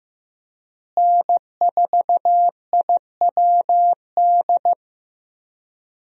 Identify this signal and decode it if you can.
Morse code, heard as a keyed tone, about 15 wpm, 700 Hz. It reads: N4IWD
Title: USA Call Signs - 5 Characters - Repeated 3x - 15wpm